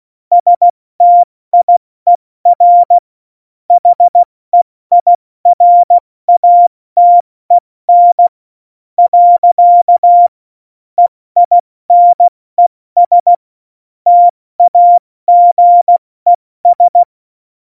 Morse code, heard as a keyed tone, about 16 wpm, 700 Hz. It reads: STIER HEIRATEN . EINES TAGES